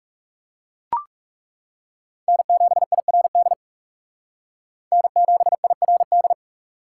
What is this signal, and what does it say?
Morse code, heard as a keyed tone, about 39 wpm, 700 Hz. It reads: N7IRD N7IRD